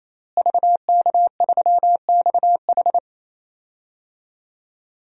Morse code, heard as a keyed tone, about 27 wpm, 700 Hz. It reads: VK3XH
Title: DXCC-100 Call Signs - 5 Characters - Repeated 3x - 28wpm